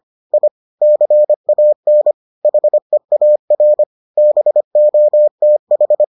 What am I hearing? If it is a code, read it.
Morse code, heard as a keyed tone, about 25 wpm, 600 Hz. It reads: I CAN HEAR BOTH